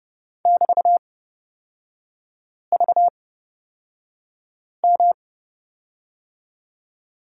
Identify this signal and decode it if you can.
Morse code, heard as a keyed tone, about 31 wpm, spaced as for 5 wpm, 700 Hz. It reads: <BT>VM
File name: Lesson 38e - Letter q - Three Characters 30wpm